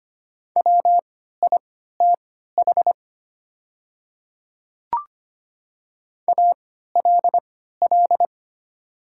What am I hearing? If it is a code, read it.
Morse code, heard as a keyed tone, about 25 wpm, spaced as for 14 wpm, 700 Hz. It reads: WITH ALL
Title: 3x Character Spacing - Top 100 Words - 25wpm